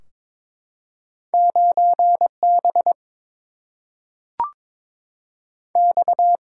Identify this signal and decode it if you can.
Morse code, heard as a keyed tone, about 22 wpm, 700 Hz. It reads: 9 B X
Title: Single-Letter Single-Number 22wpm